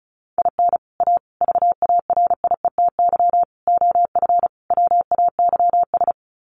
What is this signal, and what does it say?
Morse code, heard as a keyed tone, about 34 wpm, 700 Hz. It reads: IN A VARIETY OF WAYS
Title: Sets of 5 Words Encore 35wpm